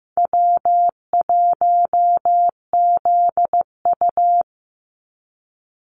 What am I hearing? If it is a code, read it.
Morse code, heard as a keyed tone, about 15 wpm, 700 Hz. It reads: W1ZU
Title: NA SOTA Activator Callsigns 15wpm